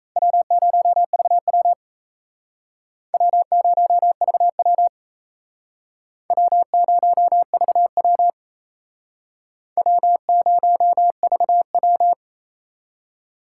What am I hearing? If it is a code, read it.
Morse code, heard as a keyed tone, about 36 wpm, 700 Hz. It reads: W0VW W0VW W0VW W0VW